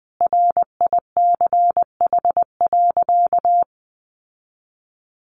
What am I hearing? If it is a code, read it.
Morse code, heard as a keyed tone, about 20 wpm, 700 Hz. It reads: RICH.
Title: Sentences from Top 1100 Words 20wpm